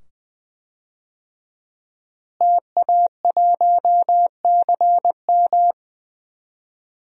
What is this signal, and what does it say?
Morse code, heard as a keyed tone, about 20 wpm, 700 Hz. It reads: TA1CM
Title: DXCC-100 Call Signs - 5 Characters - Repeated 2x - 20wpm